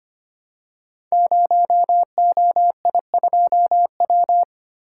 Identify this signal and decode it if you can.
Morse code, heard as a keyed tone, about 25 wpm, 700 Hz. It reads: 0OI2W
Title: Code Groups - 5 Letter:Numbers - 25wpm